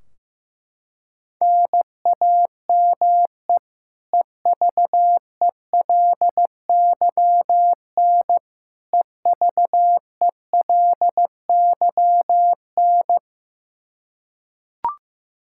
Morse code, heard as a keyed tone, about 15 wpm, 700 Hz. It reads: NAME EVELYN EVELYN
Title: Practice QSO - Exchange 1 - Beginning - 15wpm